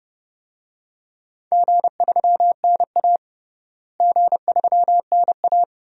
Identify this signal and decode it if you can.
Morse code, heard as a keyed tone, about 30 wpm, 700 Hz. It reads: G3NA G3NA
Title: DXCC-100 Call Signs - 4 Characters - Repeated 2x - 30wpm